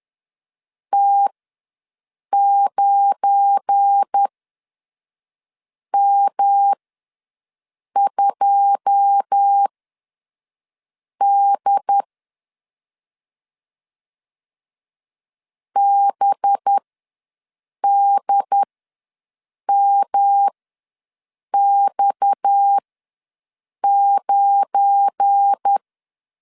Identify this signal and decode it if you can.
Morse code, heard as a keyed tone, about 10 wpm, spaced as for 5 wpm, 800 Hz. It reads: T9M2D BDMX9
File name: LEZIONI / 9 2 ws 10wpm 4130